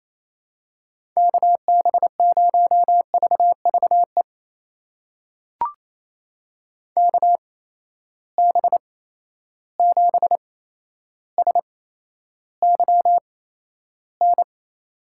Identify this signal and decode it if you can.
Morse code, heard as a keyed tone, about 28 wpm, spaced as for 12 wpm, 700 Hz. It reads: KB0VVE KB7SYN
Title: USA Call Signs - 6 Characters - 8x Character Spacing - 28wpm